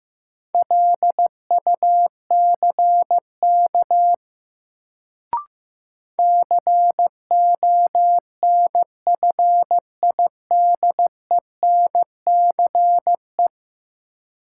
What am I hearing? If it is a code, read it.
Morse code, heard as a keyed tone, about 15 wpm, 700 Hz. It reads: LUCK CONFIDENCE
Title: Sentences from Top 1500 Words - New Word Drill Alt - 15wpm